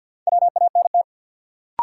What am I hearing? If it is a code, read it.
Morse code, heard as a keyed tone, about 53 wpm, 700 Hz. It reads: WANT
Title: Sets of 2 Words - New Word Drill - 50wpm